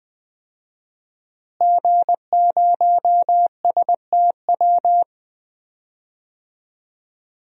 Morse code, heard as a keyed tone, about 21 wpm, 700 Hz. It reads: G0STW
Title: DXCC-100 Call Signs - 5 Characters - Repeated 3x - 20wpm